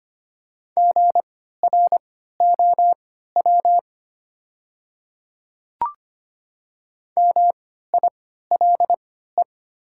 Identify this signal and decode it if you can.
Morse code, heard as a keyed tone, about 25 wpm, spaced as for 14 wpm, 700 Hz. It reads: GROW MILE
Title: 3x Character Spacing - Top 300 Words - 25wpm